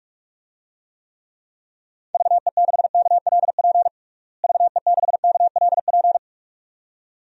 Morse code, heard as a keyed tone, about 46 wpm, 700 Hz. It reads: VE6KLP VE6KLP
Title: DXCC-100 Call Signs - 6 Characters - Repeated 2x - 45wpm